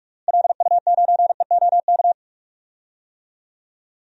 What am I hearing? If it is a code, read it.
Morse code, heard as a keyed tone, about 42 wpm, 690 Hz. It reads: LU9EOK
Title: DXCC-100 Call Signs - 6 Characters - Repeated 2x - 45wpm